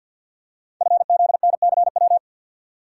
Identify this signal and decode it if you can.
Morse code, heard as a keyed tone, about 50 wpm, 690 Hz. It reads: FZNCW
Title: Code Groups - 5 Letters - 50wpm